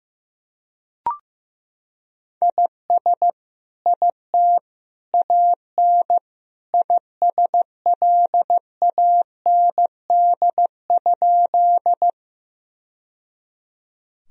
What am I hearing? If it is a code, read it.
Morse code, heard as a keyed tone, about 15 wpm, 700 Hz. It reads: IS IT AN ISLAND?
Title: Sentences from Top 1000 Words - Review Part 3 - 15wpm